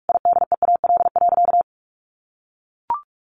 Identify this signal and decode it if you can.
Morse code, heard as a keyed tone, about 44 wpm, 700 Hz. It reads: IDEAL.
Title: Sentences from Top 1400 Words 45wpm